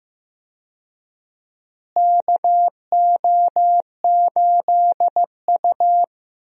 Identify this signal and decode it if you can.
Morse code, heard as a keyed tone, about 16 wpm, 700 Hz. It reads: KO8U
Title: USA Call Signs - 4 Characters - Repeated 3x - 15wpm